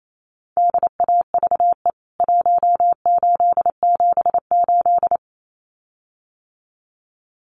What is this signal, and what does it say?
Morse code, heard as a keyed tone, about 28 wpm, 700 Hz. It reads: DAVE 1878
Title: CWT Exchange 28wpm